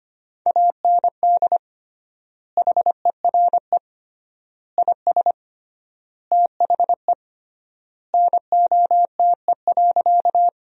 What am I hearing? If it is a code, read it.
Morse code, heard as a keyed tone, about 25 wpm, 700 Hz. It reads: AND HERE IS THE NOTE.
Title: 3x Word-Spacing - Sentences from Top 500 Words - 25wpm